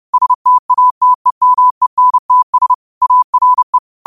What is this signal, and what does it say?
Morse code, heard as a keyed tone, about 30 wpm, 1000 Hz. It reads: STATEMENTS ARE
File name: qotdfiles / 30wpm